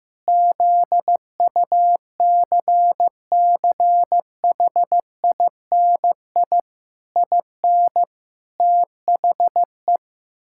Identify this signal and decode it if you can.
Morse code, heard as a keyed tone, about 16 wpm, 700 Hz. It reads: ZUCCHINI IN THE
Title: Pangram Challenge v2 - 15wpm